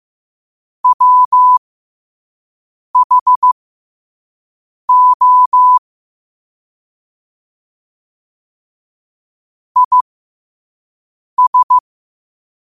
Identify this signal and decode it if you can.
Morse code, heard as a keyed tone, about 15 wpm, spaced as for 5 wpm, 1000 Hz. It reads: WHO IS
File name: qotdfiles / 5wpm